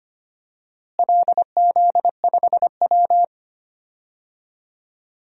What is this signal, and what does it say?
Morse code, heard as a keyed tone, about 25 wpm, 700 Hz. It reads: LZ5W
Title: DXCC-100 Call Signs - 4 Characters - Repeated 2x - 25wpm